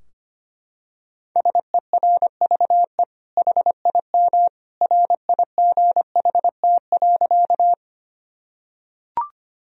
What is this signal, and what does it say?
Morse code, heard as a keyed tone, about 25 wpm, 700 Hz. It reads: SERVE HIM RIGHT.
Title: Sentences from Top 1000 Words - Review Part 2 - 25wpm